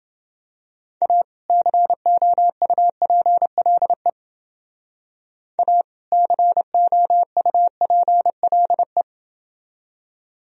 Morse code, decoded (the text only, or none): A COUPLE A COUPLE